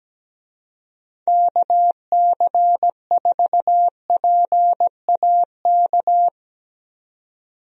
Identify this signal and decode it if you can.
Morse code, heard as a keyed tone, about 17 wpm, 700 Hz. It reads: KC4PAK
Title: USA Call Signs - 6 Characters - 17wpm